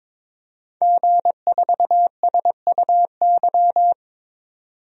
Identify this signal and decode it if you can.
Morse code, heard as a keyed tone, about 22 wpm, 700 Hz. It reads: G4SUY
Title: DXCC-100 Call Signs - 5 Characters - 22wpm